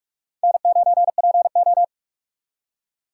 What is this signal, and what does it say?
Morse code, heard as a keyed tone, about 58 wpm, 700 Hz. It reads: N9PO